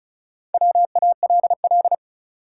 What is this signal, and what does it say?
Morse code, heard as a keyed tone, about 36 wpm, 700 Hz. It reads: WALL
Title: Sentences from Top 1000 Words - Review Part 6 - New Word Drill - 35wpm